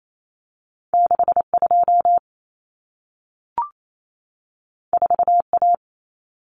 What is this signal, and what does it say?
Morse code, heard as a keyed tone, about 28 wpm, 700 Hz. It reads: 62 4A
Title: Common Cut Numbers - 2-Digit 28wpm